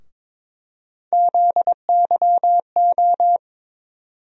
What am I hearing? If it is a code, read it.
Morse code, heard as a keyed tone, about 22 wpm, 700 Hz. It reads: ZYO
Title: USA Call Sign Postfixes - 3 Letters 22wpm